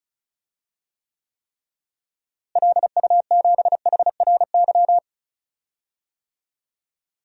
Morse code, heard as a keyed tone, about 34 wpm, 700 Hz. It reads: LU7HRY
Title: DXCC-100 Call Signs - 6 Characters - Repeated 3x - 35wpm